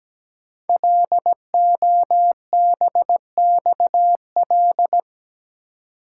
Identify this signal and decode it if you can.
Morse code, heard as a keyed tone, about 17 wpm, 700 Hz. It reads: LOBXL
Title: Code Groups - 5 Letters - 17wpm